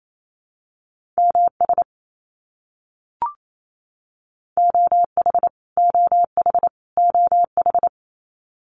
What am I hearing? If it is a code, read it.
Morse code, heard as a keyed tone, about 28 wpm, 700 Hz. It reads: MS OH OH OH
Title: US States Repeated 3x 28wpm